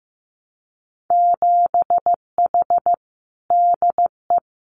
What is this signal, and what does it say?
Morse code, heard as a keyed tone, about 15 wpm, 700 Hz. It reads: 7H DE
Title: Winter Field Day Exchanges Encore 15wpm